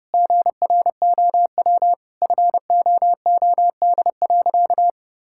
Morse code, heard as a keyed tone, about 30 wpm, 700 Hz. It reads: GROW FOOD.
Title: Sentences from Top 1000 Words - Review Part 6 - 30wpm